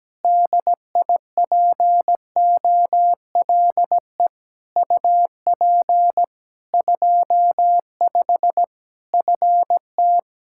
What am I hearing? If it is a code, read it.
Morse code, heard as a keyed tone, about 17 wpm, 700 Hz. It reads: DIPOLE UP 25 FT
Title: Practice QSO - Exchange 2 - Info - 17wpm